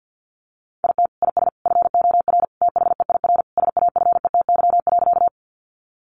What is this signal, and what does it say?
Morse code, heard as a keyed tone, about 49 wpm, 700 Hz. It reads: IT IS FOR THEIR SAFETY.
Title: Sentences from Top 700 Words 50wpm